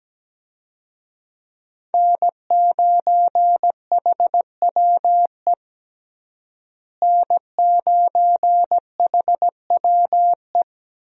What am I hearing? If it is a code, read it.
Morse code, heard as a keyed tone, about 17 wpm, 700 Hz. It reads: N9HWE N9HWE
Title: USA Call Signs - 5 Characters - Repeated 2x - 17wpm